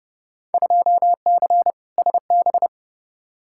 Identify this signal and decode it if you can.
Morse code, heard as a keyed tone, about 30 wpm, 700 Hz. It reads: 2C SB
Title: Field Day Exchanges Encore - 30wpm